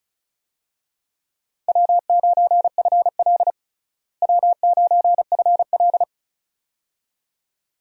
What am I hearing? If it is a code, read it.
Morse code, heard as a keyed tone, about 36 wpm, 700 Hz. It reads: W9FL W9FL
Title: USA Call Signs - 4 Characters - Repeated 2x - 35wpm